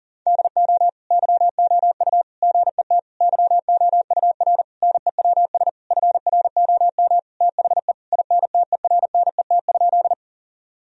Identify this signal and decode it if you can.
Morse code, heard as a keyed tone, about 37 wpm, 700 Hz. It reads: DO YOU GET YOUR NEWS FROM THE INTERNET?
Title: Sentences from Top 1000 Words - Review Part 4 - 40wpm